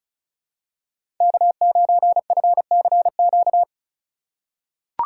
Morse code, heard as a keyed tone, about 36 wpm, 700 Hz. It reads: K9FCQ
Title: USA Call Signs - 5 Characters - 4x Character Spacing - 35wpm